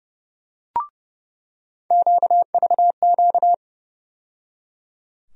Morse code, heard as a keyed tone, about 30 wpm, 700 Hz. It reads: QVQ
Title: USA Call Sign Postfixes - 3 Letters 30wpm